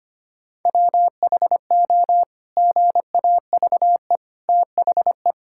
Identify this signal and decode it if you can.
Morse code, heard as a keyed tone, about 25 wpm, 700 Hz. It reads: WHO GAVE THE
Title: Sentences from Top 1000 Words - Review Part 6 - 25wpm